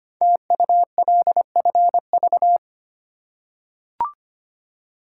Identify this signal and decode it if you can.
Morse code, heard as a keyed tone, about 25 wpm, 700 Hz. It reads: TULFV
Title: Code Groups - 5 Letters - 25wpm